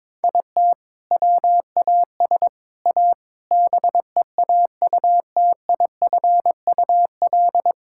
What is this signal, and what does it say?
Morse code, heard as a keyed tone, about 21 wpm, 700 Hz. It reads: IT WAS A BEAUTIFUL
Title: Sentences from Top 1000 Words - Review Part 5 - 22wpm